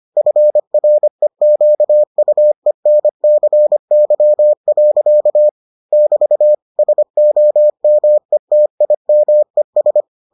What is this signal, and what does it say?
Morse code, heard as a keyed tone, about 25 wpm, 600 Hz. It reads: FREQUENCY. <BT> SOMETIMES